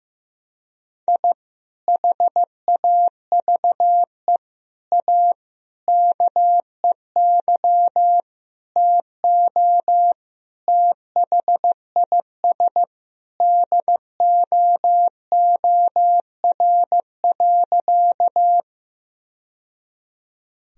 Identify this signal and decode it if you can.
Morse code, heard as a keyed tone, about 15 wpm, 700 Hz. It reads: I HAVE A KEY TO THIS DOOR.
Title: Sentences from Top 1000 Words - Review Part 4 - 15wpm